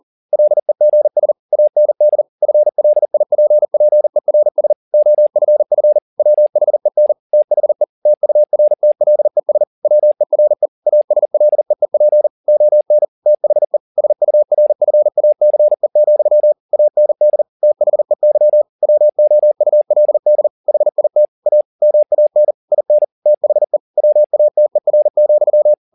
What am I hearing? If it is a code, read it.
Morse code, heard as a keyed tone, about 40 wpm, 600 Hz. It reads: LEGS AND FLIPPERS OFF WHEN THE TURTLES WERE ASLEEP ON THE SURFACE, AND THEY WOULD HIT A MAN IN THE WATER,
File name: the old man and the sea 2 40wpm